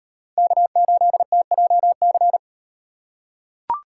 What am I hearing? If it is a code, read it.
Morse code, heard as a keyed tone, about 35 wpm, 700 Hz. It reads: K8TJC